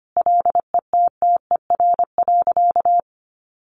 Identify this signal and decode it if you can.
Morse code, heard as a keyed tone, about 25 wpm, 700 Hz. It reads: LETTER.
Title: Sentences from Top 1400 Words 25wpm